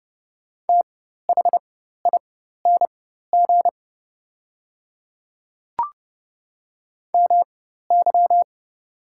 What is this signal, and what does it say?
Morse code, heard as a keyed tone, about 30 wpm, spaced as for 14 wpm, 700 Hz. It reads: THING MY
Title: 4x Character Spacing - Top 100 Words - 30wpm